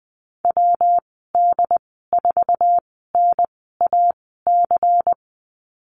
Morse code, heard as a keyed tone, about 20 wpm, spaced as for 14 wpm, 700 Hz. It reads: WD4NAC
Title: 179 Lesson 29g - Slash - Call Signs 20wpm